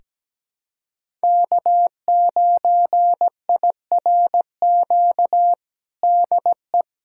Morse code, heard as a keyed tone, about 17 wpm, 700 Hz. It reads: K9IRQ DE
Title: USA Callsign DE Callsign - 5 Characters - 17wpm